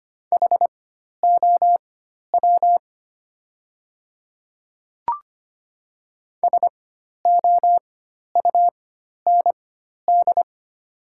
Morse code, heard as a keyed tone, about 25 wpm, spaced as for 12 wpm, 690 Hz. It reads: HOW SOUND
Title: 4x Character Spacing - Top 100 Words - 25wpm